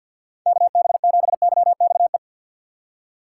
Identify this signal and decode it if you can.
Morse code, heard as a keyed tone, about 51 wpm, 690 Hz. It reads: KB7YXE
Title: USA Call Signs - 6 Characters - 8x Character Spacing - 50wpm